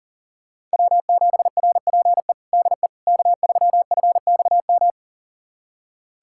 Morse code, heard as a keyed tone, about 40 wpm, 700 Hz. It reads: W7RPE DE K3FXM
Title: USA Callsign DE Callsign - 5 Characters - 40wpm